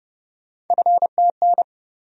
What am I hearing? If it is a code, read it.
Morse code, heard as a keyed tone, about 30 wpm, 700 Hz. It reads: FTN
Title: USA Call Sign Postfixes - 3 Letters 30wpm